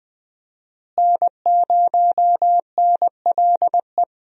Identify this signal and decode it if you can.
Morse code, heard as a keyed tone, about 20 wpm, 700 Hz. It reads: N0NLE
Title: USA Call Signs - 5 Characters - Repeated 3x - 20wpm